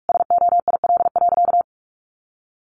Morse code, heard as a keyed tone, about 44 wpm, 700 Hz. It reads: SOIL.
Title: Sentences from Top 1500 Words 45wpm